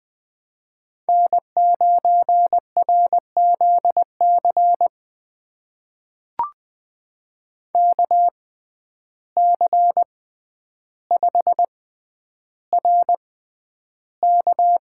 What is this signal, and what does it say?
Morse code, heard as a keyed tone, about 20 wpm, spaced as for 10 wpm, 700 Hz. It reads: N9RZC KC5RK
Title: USA Call Signs - 5 Characters - 6x Character Spacing - 20wpm